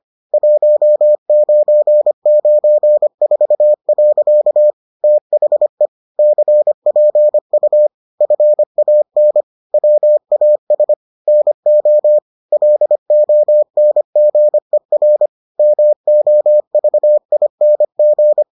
Morse code, heard as a keyed tone, about 25 wpm, 600 Hz. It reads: 1994. THE CPU FAN WAS NO LONGER MOVING